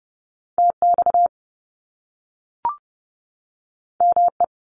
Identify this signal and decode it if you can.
Morse code, heard as a keyed tone, about 30 wpm, 700 Hz. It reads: TX ME
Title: US States 30wpm